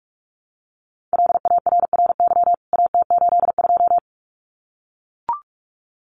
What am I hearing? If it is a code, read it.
Morse code, heard as a keyed tone, about 44 wpm, 700 Hz. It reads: LARRY AT82
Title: CWT Exchange 45wpm